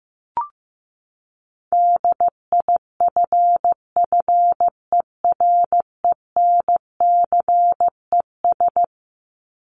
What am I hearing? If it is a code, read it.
Morse code, heard as a keyed tone, about 15 wpm, 700 Hz. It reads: DIFFERENCES
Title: ES:S Suffix - Top 100 Words with Base Words 15wpm